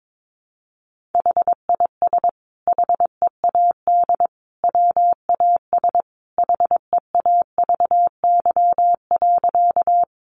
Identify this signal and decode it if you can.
Morse code, heard as a keyed tone, about 22 wpm, 700 Hz. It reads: HIS HEAD WAS HEAVY.
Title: Sentences from Top 1100 Words 22wpm